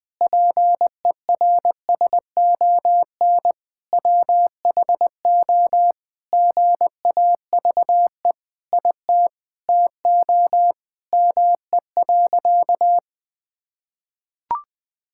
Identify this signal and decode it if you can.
Morse code, heard as a keyed tone, about 21 wpm, 700 Hz. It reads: PERSON WHO GAVE IT TO ME.
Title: Sentences from Top 1600 Words 20wpm